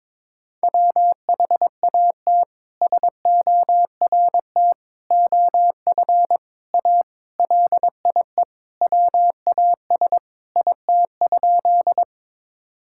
Morse code, heard as a keyed tone, about 22 wpm, 700 Hz. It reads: WHAT SORT OF A LIE WAS IT?
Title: Sentences from Top 1600 Words 22wpm